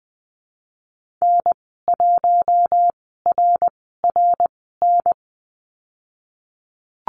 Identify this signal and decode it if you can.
Morse code, heard as a keyed tone, about 20 wpm, spaced as for 15 wpm, 700 Hz. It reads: N1RRN
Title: 055 Lesson 10g - Letter r - Call Signs 20wpm